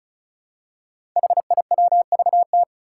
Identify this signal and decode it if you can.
Morse code, heard as a keyed tone, about 36 wpm, 700 Hz. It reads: HIWVT
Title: Five Challenging Letter-Number Combinations 35wpm